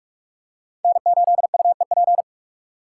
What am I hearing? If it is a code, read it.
Morse code, heard as a keyed tone, about 44 wpm, 700 Hz. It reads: N8UEP
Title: Code Groups - 5 Letter:Numbers - 45wpm